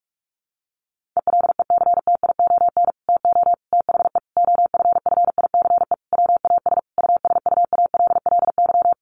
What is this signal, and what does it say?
Morse code, heard as a keyed tone, about 44 wpm, 700 Hz. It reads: ELECTION TO THE OFFICE WAS USUALLY
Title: Sentences from Top 1500 Words 45wpm